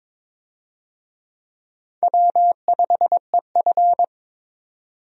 Morse code, heard as a keyed tone, about 22 wpm, 700 Hz. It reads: W5EF
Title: USA Call Signs - 4 Characters - 22wpm